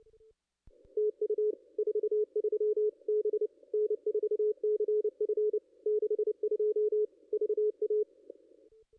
Morse code, heard as a keyed tone, about 27 wpm, 410 Hz. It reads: TU 43 B N4CF 62 VA
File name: contests / ss-cw-k5tr-op-n5rz -2200